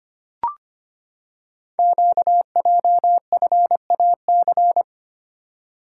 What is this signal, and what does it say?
Morse code, heard as a keyed tone, about 25 wpm, 700 Hz. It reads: QJFAC
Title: Code Groups - 5 Letters - 25wpm